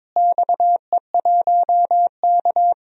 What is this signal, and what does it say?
Morse code, heard as a keyed tone, about 22 wpm, 700 Hz. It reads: XE1K
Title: DXCC-100 Call Signs - 4 Characters - Repeated 2x - 22wpm